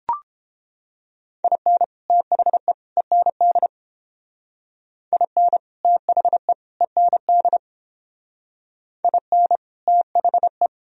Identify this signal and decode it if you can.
Morse code, heard as a keyed tone, about 31 wpm, 700 Hz. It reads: IN THE END IN THE END IN THE